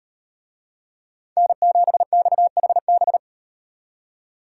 Morse code, heard as a keyed tone, about 38 wpm, 700 Hz. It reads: N7XHB